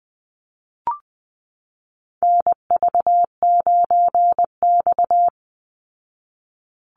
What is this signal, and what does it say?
Morse code, heard as a keyed tone, about 20 wpm, 700 Hz. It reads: NV9X
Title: Call Signs Repeated 2x - 20wpm